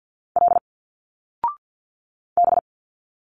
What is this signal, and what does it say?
Morse code, heard as a keyed tone, about 50 wpm, 700 Hz. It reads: L B
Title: Single Letters 50wpm